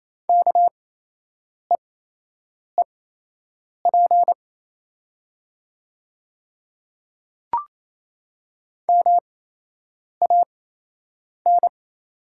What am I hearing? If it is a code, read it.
Morse code, heard as a keyed tone, about 28 wpm, spaced as for 8 wpm, 690 Hz. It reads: KEEP MAN
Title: 8x Character Spacing - Top 500 Words - 28wpm